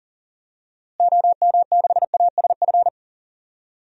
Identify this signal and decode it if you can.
Morse code, heard as a keyed tone, about 40 wpm, 700 Hz. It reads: OM6ASF
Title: DXCC-100 Call Signs - 6 Characters - Repeated 3x - 40wpm